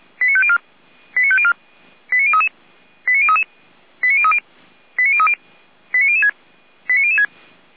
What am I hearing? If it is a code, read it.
Five-tone selective call (ZVEI), heard as quick runs of five tones, tones of about 70 ms. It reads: 89504 89504 89033 89033 89030 89030 89006 89006